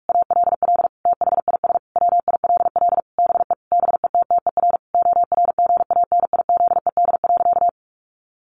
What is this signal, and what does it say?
Morse code, heard as a keyed tone, about 44 wpm, 700 Hz. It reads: ALL THIS WILL BE BETTER ORGANIZED.
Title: Sentences from Top 1300 Words 45wpm